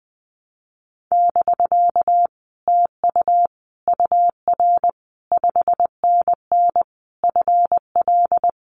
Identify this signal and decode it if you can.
Morse code, heard as a keyed tone, about 20 wpm, 700 Hz. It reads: <BK> TU UR 5NN FL
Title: Common QSO Phrases Encore 20wpm